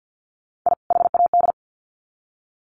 Morse code, heard as a keyed tone, about 50 wpm, 700 Hz. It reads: I HAD
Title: Sets of 2 Words 50wpm